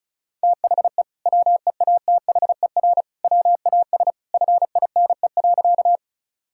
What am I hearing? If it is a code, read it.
Morse code, heard as a keyed tone, about 36 wpm, 700 Hz. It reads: THE WEATHER WAS FINE.
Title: Sentences from Top 1000 Words - Review Part 5 - 35wpm